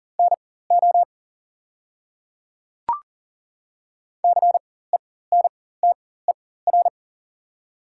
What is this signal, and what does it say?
Morse code, heard as a keyed tone, about 40 wpm, spaced as for 19 wpm, 700 Hz. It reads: NO CENTER
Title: 4x Character Spacing - Top 500 Words - 40wpm